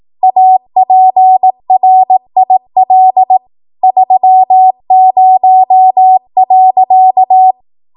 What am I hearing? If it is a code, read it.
Morse code, heard as a keyed tone, about 18 wpm, 750 Hz. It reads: APRIL 30.
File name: Morse / 18WPM